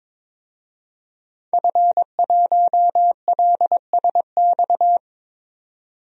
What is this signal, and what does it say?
Morse code, heard as a keyed tone, about 22 wpm, 700 Hz. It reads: F1LSX
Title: DXCC-100 Call Signs - 5 Characters - 22wpm